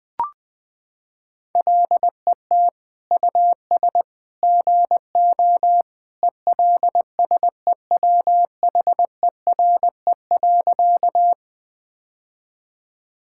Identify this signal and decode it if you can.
Morse code, heard as a keyed tone, about 20 wpm, 700 Hz. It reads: LET US GO ELSEWHERE.
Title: Sentences from Top 1600 Words 20wpm